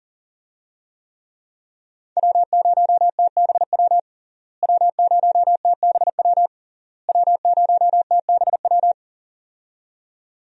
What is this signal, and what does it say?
Morse code, heard as a keyed tone, about 40 wpm, 700 Hz. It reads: W0TBW W0TBW W0TBW
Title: USA Call Signs - 5 Characters - Repeated 3x - 40wpm